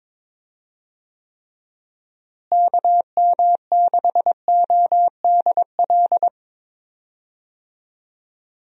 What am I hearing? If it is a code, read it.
Morse code, heard as a keyed tone, about 22 wpm, 700 Hz. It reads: KM6ODL
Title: USA Call Signs - 6 Characters - Repeated 2x - 22wpm